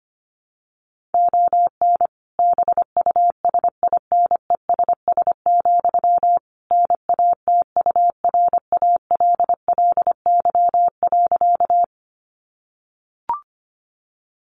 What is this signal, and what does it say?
Morse code, heard as a keyed tone, about 25 wpm, 700 Hz. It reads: ON BUSINESS, NATURALLY.
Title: Sentences from Top 1300 Words 25wpm